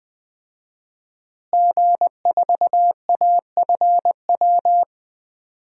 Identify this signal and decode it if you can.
Morse code, heard as a keyed tone, about 20 wpm, 700 Hz. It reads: G4AFW
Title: DXCC-100 Call Signs - 5 Characters - Repeated 3x - 20wpm